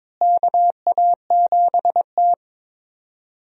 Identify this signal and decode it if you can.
Morse code, heard as a keyed tone, about 22 wpm, 700 Hz. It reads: KA7T
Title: USA Call Signs - 4 Characters - Repeated 2x - 22wpm